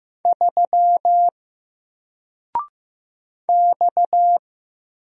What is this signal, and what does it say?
Morse code, heard as a keyed tone, about 15 wpm, 700 Hz. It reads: 3 X
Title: Single Letter-Number 15wpm